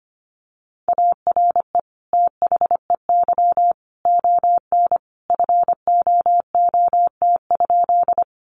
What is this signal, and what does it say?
Morse code, heard as a keyed tone, about 25 wpm, 700 Hz. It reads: ARE THEY ON FOOT?
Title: Sentences from Top 500 Words 25wpm